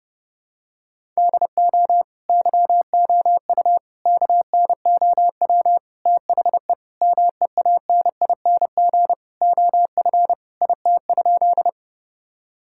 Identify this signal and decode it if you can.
Morse code, heard as a keyed tone, about 32 wpm, 700 Hz. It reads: DO YOU KNOW THE MEANING OF IT?
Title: Sentences from Top 1000 Words - Review Part 6 - 30wpm